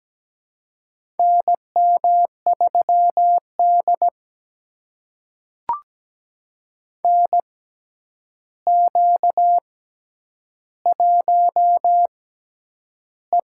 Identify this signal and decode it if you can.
Morse code, heard as a keyed tone, about 17 wpm, spaced as for 9 wpm, 700 Hz. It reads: NM3D NQ1E
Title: USA Call Signs - 4 Characters - 6x Character Spacing - 17wpm